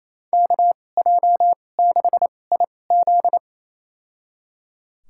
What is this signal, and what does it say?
Morse code, heard as a keyed tone, about 28 wpm, spaced as for 20 wpm, 700 Hz. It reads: KJ6IZ
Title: USA Call Signs - 2x Character Spacing - 28wpm